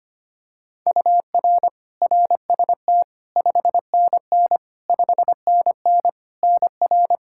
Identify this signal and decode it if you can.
Morse code, heard as a keyed tone, about 25 wpm, 700 Hz. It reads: UR RST 5NN 5NN NR
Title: Practice QSO - Exchange 1 - Beginning - 25wpm